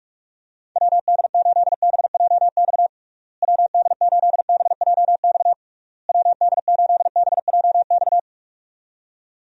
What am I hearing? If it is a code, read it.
Morse code, heard as a keyed tone, about 46 wpm, 700 Hz. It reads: WD8BJX WD8BJX WD8BJX
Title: USA Call Signs - 6 Characters - Repeated 3x - 45wpm